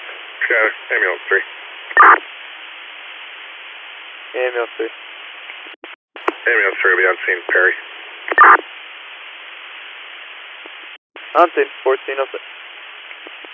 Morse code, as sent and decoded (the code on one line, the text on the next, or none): none